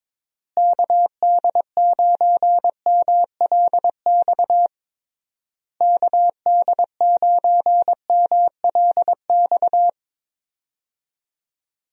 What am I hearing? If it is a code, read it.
Morse code, heard as a keyed tone, about 21 wpm, 700 Hz. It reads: KD9MLX KD9MLX
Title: USA Call Signs - 6 Characters - Repeated 3x - 22wpm